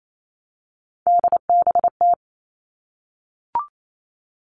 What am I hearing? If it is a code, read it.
Morse code, heard as a keyed tone, about 28 wpm, 700 Hz. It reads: DBT
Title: All Cut Numbers - 3-Digit 28wpm